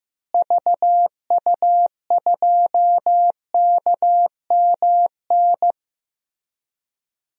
Morse code, heard as a keyed tone, about 15 wpm, 700 Hz. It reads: VU2KMN
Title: DXCC-100 Call Signs - 6 Characters - Repeated 2x - 15wpm